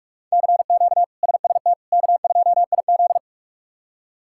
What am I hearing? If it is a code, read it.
Morse code, heard as a keyed tone, about 46 wpm, 700 Hz. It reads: CQ SST K2IZ
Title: K1USN SST QSO Practice - 45wpm